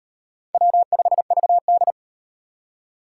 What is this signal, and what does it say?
Morse code, heard as a keyed tone, about 33 wpm, 700 Hz. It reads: W5VD